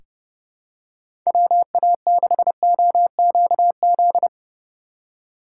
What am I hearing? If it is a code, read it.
Morse code, heard as a keyed tone, about 30 wpm, 700 Hz. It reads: WA6OQZ
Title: USA Call Signs - 6 Characters - 2x Character Spacing - 30wpm